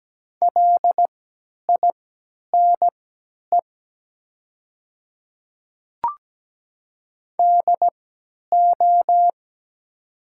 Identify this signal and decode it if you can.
Morse code, heard as a keyed tone, about 17 wpm, spaced as for 10 wpm, 700 Hz. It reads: LINE DO
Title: 3x Character Spacing - Top 300 Words - 17wpm